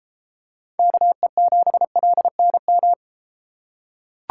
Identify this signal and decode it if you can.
Morse code, heard as a keyed tone, about 31 wpm, 700 Hz. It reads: KE7LNM